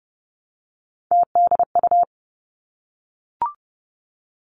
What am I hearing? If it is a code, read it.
Morse code, heard as a keyed tone, about 30 wpm, 700 Hz. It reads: TDU
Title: Three Challenging Letter-Number Combinations 30wpm